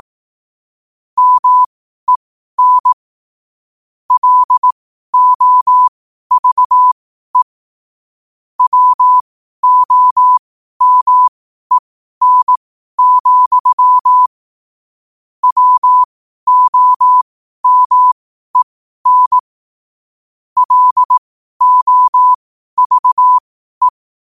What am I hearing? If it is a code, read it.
Morse code, heard as a keyed tone, about 18 wpm, spaced as for 12 wpm, 1000 Hz. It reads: MEN LOVE WOMEN, WOMEN LOVE
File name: qotdfiles / 12wpm